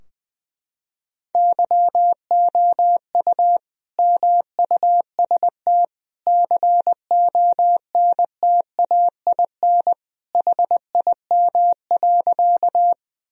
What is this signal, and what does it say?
Morse code, heard as a keyed tone, about 21 wpm, 700 Hz. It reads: YOU MUST CONTAIN HIM.
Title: Sentences from Top 1000 Words - Review Part 2 - 20wpm